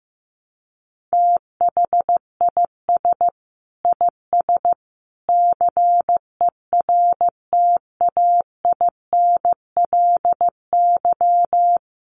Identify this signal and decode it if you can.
Morse code, heard as a keyed tone, about 15 wpm, 700 Hz. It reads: THIS IS CERTAINLY
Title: Sentences from Top 1300 Words 15wpm